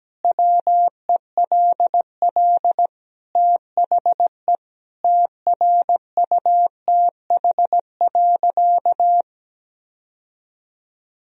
Morse code, heard as a keyed tone, about 17 wpm, 700 Hz. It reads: WELL THE TRUTH.
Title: Sentences from Top 1000 Words - Review Part 3 - 17wpm